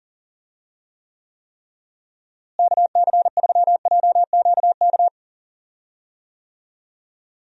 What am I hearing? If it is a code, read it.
Morse code, heard as a keyed tone, about 37 wpm, 700 Hz. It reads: KC3JQK
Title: USA Call Signs - 6 Characters - Repeated 3x - 40wpm